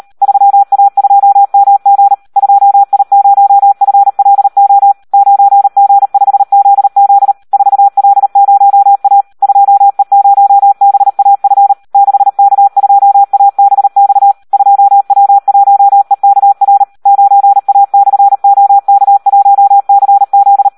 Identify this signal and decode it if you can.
Morse code, heard as a keyed tone, about 38 wpm, 800 Hz. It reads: 3A2MG 2I0FLO 9G5ZZ 4L0A 2E0BAF 6K2ABX 2W1EKR 9A/OK1CZ
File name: Audio practica CW / SD 1 38